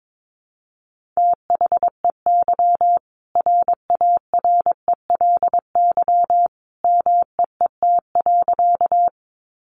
Morse code, heard as a keyed tone, about 22 wpm, 700 Hz. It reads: THEY RARELY MEET.
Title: Sentences from Top 1400 Words 22wpm